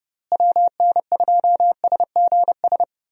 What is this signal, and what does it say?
Morse code, heard as a keyed tone, about 30 wpm, 700 Hz. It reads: WN2SGS
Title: USA Call Signs - 6 Characters - Repeated 3x - 30wpm